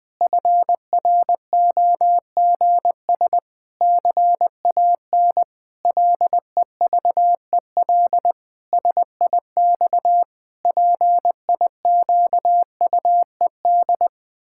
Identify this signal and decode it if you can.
Morse code, heard as a keyed tone, about 20 wpm, 700 Hz. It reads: FROGS CAN LEVEL SIX PIQUED
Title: Pangram Challenge v2 - 20wpm